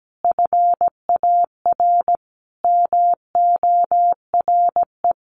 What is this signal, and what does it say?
Morse code, heard as a keyed tone, about 17 wpm, 700 Hz. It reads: FAR MORE
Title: Sentences from Top 1200 Words 17wpm